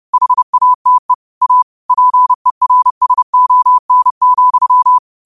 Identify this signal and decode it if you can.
Morse code, heard as a keyed tone, about 30 wpm, 1000 Hz. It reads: HATE A PERSON,